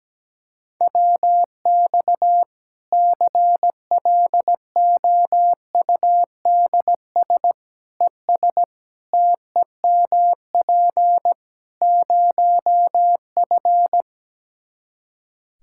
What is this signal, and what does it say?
Morse code, heard as a keyed tone, about 17 wpm, 700 Hz. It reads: WX CLOUDS ES TEMP 0F
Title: Practice QSO - Exchange 2 - Info - 17wpm